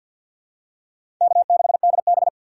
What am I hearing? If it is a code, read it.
Morse code, heard as a keyed tone, about 52 wpm, 700 Hz. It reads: K6DB
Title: USA Call Signs - 4 Characters - 3x Character Spacing - 50wpm